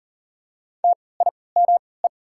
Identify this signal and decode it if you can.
Morse code, heard as a keyed tone, about 38 wpm, spaced as for 22 wpm, 700 Hz. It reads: TIME
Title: 3x Character Spacing - Top 200 Words - 40wpm